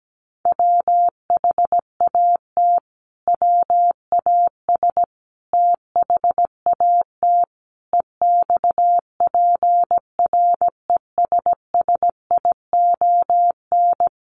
Morse code, heard as a keyed tone, about 17 wpm, 700 Hz. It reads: WHAT WAS THAT EXPRESSION
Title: Sentences from Top 1200 Words 17wpm